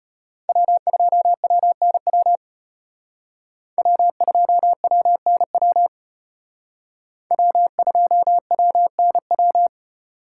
Morse code, heard as a keyed tone, about 32 wpm, 700 Hz. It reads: W2WNW W2WNW W2WNW